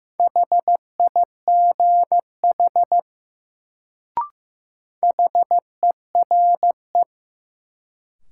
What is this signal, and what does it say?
Morse code, heard as a keyed tone, about 15 wpm, 700 Hz. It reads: HIGH HERE
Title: Sentences from Top 200 Words - New Word Drill - 15wpm